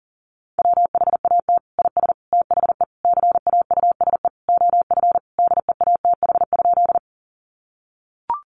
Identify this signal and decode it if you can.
Morse code, heard as a keyed tone, about 40 wpm, 700 Hz. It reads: WHAT IS THE CAUSE OF DEATH?
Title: Sentences from Top 1100 Words 40wpm